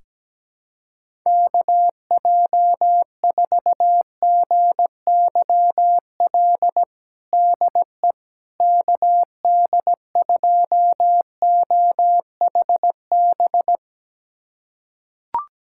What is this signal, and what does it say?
Morse code, heard as a keyed tone, about 17 wpm, 700 Hz. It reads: KJ4GYL DE KD2OHB
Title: USA Callsign DE Callsign - 6 Characters - 17wpm